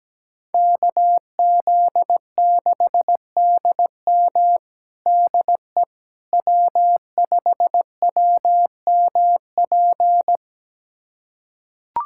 Morse code, heard as a keyed tone, about 17 wpm, 700 Hz. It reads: KZ6DM DE W5WMP
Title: USA Callsign DE Callsign - 5 Characters - 17wpm